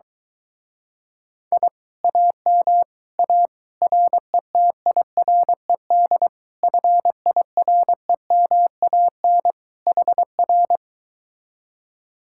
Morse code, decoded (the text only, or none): I AM A RETIRED FIREMAN HR